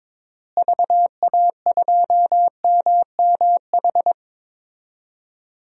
Morse code, heard as a keyed tone, about 22 wpm, 700 Hz. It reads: VA2MMH
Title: DXCC-100 Call Signs - 6 Characters - Repeated 2x - 22wpm